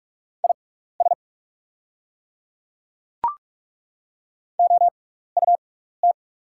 Morse code, heard as a keyed tone, about 43 wpm, spaced as for 15 wpm, 690 Hz. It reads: IS OUT
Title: 6x Character Spacing - Top 100 Words - 45wpm